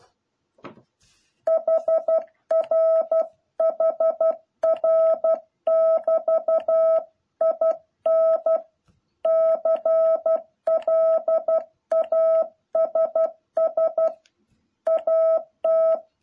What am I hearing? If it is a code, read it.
Morse code, heard as a keyed tone, about 12 wpm, 650 Hz. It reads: HR HR<BT>IN CLASS AT
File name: jh8cbh / 2021cw